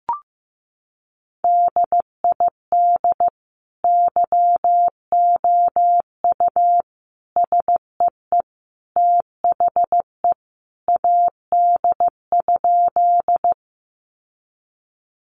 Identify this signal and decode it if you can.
Morse code, heard as a keyed tone, about 15 wpm, 700 Hz. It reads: DID YOU SEE THE AD?
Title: Sentences from Top 1300 Words 15wpm